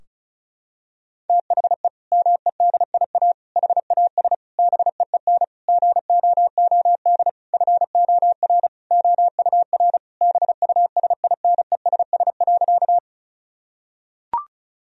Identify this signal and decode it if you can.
Morse code, heard as a keyed tone, about 36 wpm, 700 Hz. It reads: THE MEDIA HAS BEEN GOOD FOR OUR BUSINESS.
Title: Sentences from Top 1000 Words - Review Part 4 - 35wpm